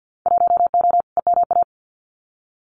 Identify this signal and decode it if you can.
Morse code, heard as a keyed tone, about 50 wpm, 700 Hz. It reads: 1O EMA
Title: Winter Field Day Exchanges 50wpm